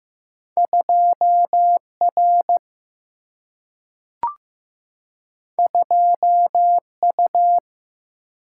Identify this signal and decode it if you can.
Morse code, heard as a keyed tone, about 15 wpm, 700 Hz. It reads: 2R 2U
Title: Single-Letter Single-Number 15wpm